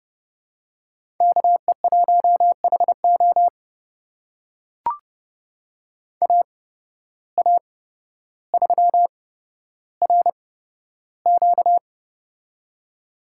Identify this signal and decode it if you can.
Morse code, heard as a keyed tone, about 31 wpm, spaced as for 13 wpm, 700 Hz. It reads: KE1HO AA3RQ
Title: USA Call Signs - 5 Characters - 8x Character Spacing - 30wpm